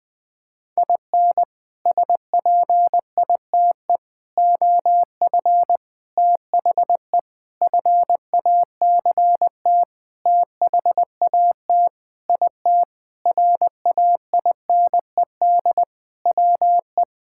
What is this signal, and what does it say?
Morse code, heard as a keyed tone, about 20 wpm, 700 Hz. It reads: IN SPITE OF THE FACT THAT IT RAINED WE
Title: Sentences with 6-Word Phrases - Spoken Beginning - 20wpm